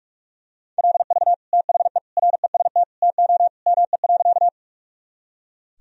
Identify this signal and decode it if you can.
Morse code, heard as a keyed tone, about 43 wpm, 700 Hz. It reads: LV THE REST TO ME.
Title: Sentences from Top 500 Words and Abbreviations 45wpm